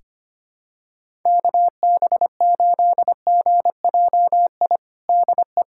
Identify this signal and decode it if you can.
Morse code, heard as a keyed tone, about 25 wpm, 700 Hz. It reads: KB8GJI DE
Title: USA Callsign DE Callsign 25wpm